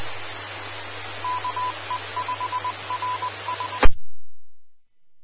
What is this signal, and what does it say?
Morse code, heard as a keyed tone, about 20 wpm, 1000 Hz. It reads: KE5RS
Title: Austin, TX Amateur Repeaters